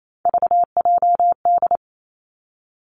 Morse code, heard as a keyed tone, about 28 wpm, 690 Hz. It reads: VJD
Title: Three Challenging Letter-Number Combinations 28wpm